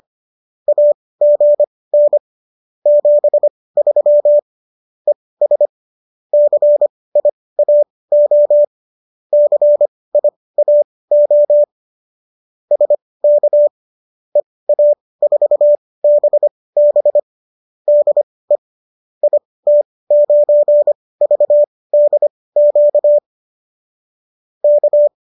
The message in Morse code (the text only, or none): AGN 73 ES CIAO CIAO SK EA4BB DE IT9VDQ K